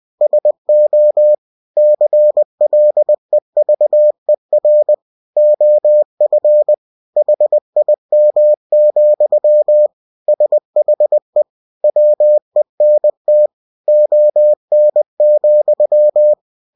Morse code, heard as a keyed tone, about 20 wpm, 600 Hz. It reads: SO CLEVER OF HIM, SHE WENT ON,
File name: cw-books / winnie-the-pooh / winnie-20wpm-page-0072